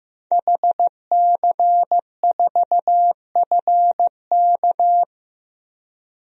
Morse code, heard as a keyed tone, about 15 wpm, 700 Hz. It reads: HC4FK
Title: DXCC-100 Call Signs - 5 Characters - 15wpm